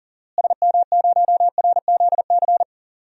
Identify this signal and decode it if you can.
Morse code, heard as a keyed tone, about 39 wpm, 700 Hz. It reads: SM0RZC
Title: DXCC-100 Call Signs - 6 Characters - Repeated 2x - 40wpm